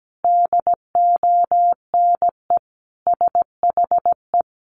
Morse code, heard as a keyed tone, about 17 wpm, 700 Hz. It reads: DONE SHE
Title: Sentences from Top 900 Words 17wpm